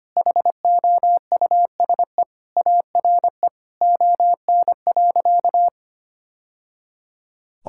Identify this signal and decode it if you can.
Morse code, heard as a keyed tone, about 25 wpm, 700 Hz. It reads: HOUSE ARE ON.
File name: practice / 25wpm / Sentences with 2-Word Phrases - 25wpm